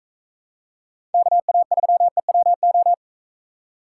Morse code, heard as a keyed tone, about 45 wpm, 700 Hz. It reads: KA3EWO